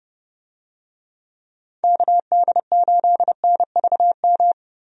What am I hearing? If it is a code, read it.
Morse code, heard as a keyed tone, about 32 wpm, 700 Hz. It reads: KD8NVM
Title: USA Call Signs - 6 Characters - Repeated 2x - 30wpm